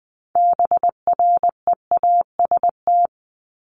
Morse code, heard as a keyed tone, about 20 wpm, 700 Hz. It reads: BREAST
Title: Top 1751-2000 Words 20wpm